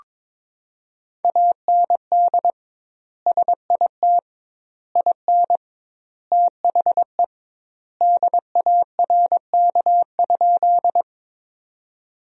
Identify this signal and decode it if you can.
Morse code, heard as a keyed tone, about 22 wpm, 700 Hz. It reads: AND SIT IN THE DARK?
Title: 2x Word-Spacing - Sentences from Top 500 Words - 22wpm